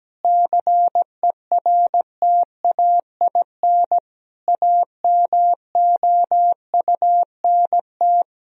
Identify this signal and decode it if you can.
Morse code, heard as a keyed tone, about 17 wpm, 700 Hz. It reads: CERTAIN AMOUNT
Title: Sentences from Top 1000 Words - Review Part 4 - 17wpm